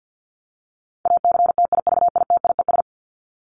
Morse code, heard as a keyed tone, about 50 wpm, 700 Hz. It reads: ACTIVITIES
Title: ES:S Suffix - Top 100 Words with Base Words 50wpm